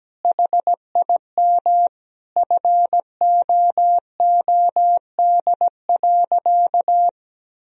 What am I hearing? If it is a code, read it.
Morse code, heard as a keyed tone, about 17 wpm, 700 Hz. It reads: HIM FOOD.
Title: Sentences from Top 1000 Words - Review Part 1 - 17wpm